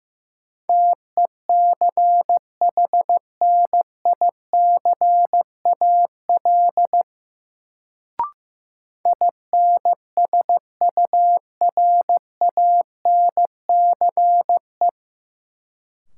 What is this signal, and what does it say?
Morse code, heard as a keyed tone, about 15 wpm, 700 Hz. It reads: TECHNICAL INSURANCE
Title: Sentences from Top 1100 Words - New Word Drill Alt - 15wpm